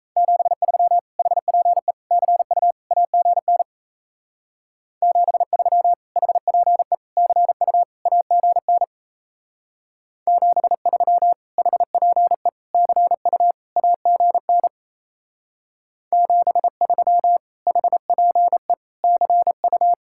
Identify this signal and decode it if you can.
Morse code, heard as a keyed tone, about 36 wpm, 700 Hz. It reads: 73 HPE CU AGN 73 HPE CU AGN 73 HPE CU AGN 73 HPE CU